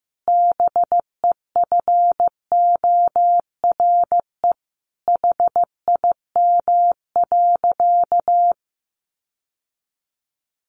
Morse code, heard as a keyed tone, about 14 wpm, 700 Hz. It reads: BEFORE HIM.
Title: Sentences from Top 1400 Words 15wpm